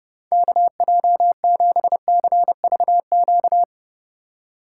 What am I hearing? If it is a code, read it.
Morse code, heard as a keyed tone, about 30 wpm, 700 Hz. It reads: KJ7CVQ